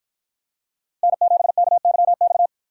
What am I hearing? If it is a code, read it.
Morse code, heard as a keyed tone, about 53 wpm, 700 Hz. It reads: N7KYX